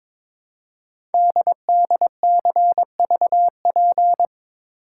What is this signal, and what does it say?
Morse code, heard as a keyed tone, about 22 wpm, 700 Hz. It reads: DDCVP
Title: Code Groups - 5 Letters - 22wpm